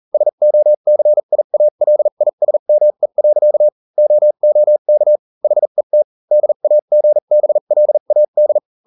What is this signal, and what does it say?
Morse code, heard as a keyed tone, about 40 wpm, spaced as for 35 wpm, 600 Hz. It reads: SOCIALISME. OOK HET DAGBLAD